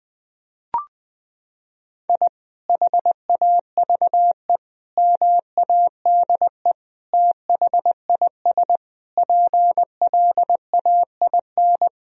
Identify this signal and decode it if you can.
Morse code, heard as a keyed tone, about 20 wpm, 700 Hz. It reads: I HAVE MADE THIS PLAIN
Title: Sentences from Top 1000 Words - Review Part 2 - 20wpm